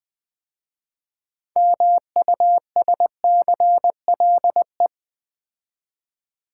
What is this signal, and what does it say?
Morse code, heard as a keyed tone, about 20 wpm, 700 Hz. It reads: MUSCLE
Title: 6 Letter Words - First 3 Letters Given - 20wpm